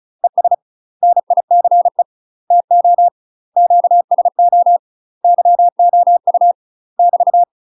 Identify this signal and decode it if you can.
Morse code, heard as a keyed tone, about 34 wpm, 700 Hz. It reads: ES NICE TO QSO YOU <BT>
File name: UK MASTER QSOs TRAINING / 35WPM / UK MASTER QSOs 0025